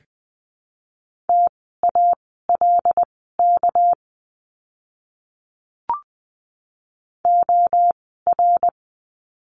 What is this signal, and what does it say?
Morse code, heard as a keyed tone, about 20 wpm, spaced as for 14 wpm, 700 Hz. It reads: TALK OR
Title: 206 Lesson 33f - Letter j - Words 20wpm